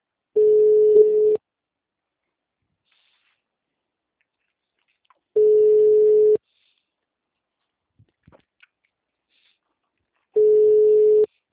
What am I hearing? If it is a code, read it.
Ringback tone (Europe, 425 Hz).